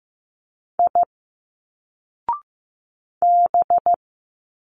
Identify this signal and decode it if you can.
Morse code, heard as a keyed tone, about 15 wpm, 700 Hz. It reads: I B
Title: Single Letters 15wpm